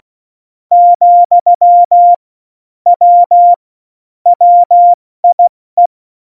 Morse code, heard as a keyed tone, about 16 wpm, 700 Hz. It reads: , W WIE
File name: Morsekurs / Witze / Witz 05. Juni 2018, 70 F80